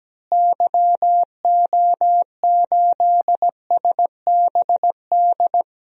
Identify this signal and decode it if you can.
Morse code, heard as a keyed tone, about 17 wpm, 700 Hz. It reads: YO8SBD
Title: DXCC-100 Call Signs - 6 Characters - Repeated 3x - 17wpm